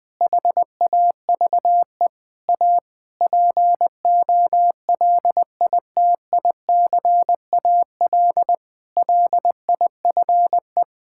Morse code, heard as a keyed tone, about 20 wpm, 700 Hz. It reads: HAVE A POLITICAL LIFE
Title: Sentences from Top 1000 Words - Review Part 5 - 20wpm